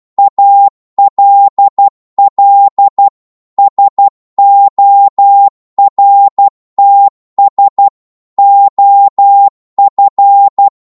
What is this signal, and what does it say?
Morse code, heard as a keyed tone, about 12 wpm, 800 Hz. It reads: ALL SORTS OF
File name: 12wpm